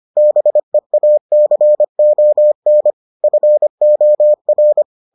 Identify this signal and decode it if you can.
Morse code, heard as a keyed tone, about 25 wpm, 600 Hz. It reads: BEACON FOR